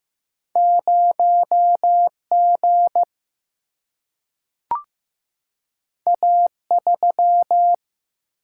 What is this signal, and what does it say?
Morse code, heard as a keyed tone, about 15 wpm, 700 Hz. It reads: 0G A3
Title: Single-Letter Single-Number 15wpm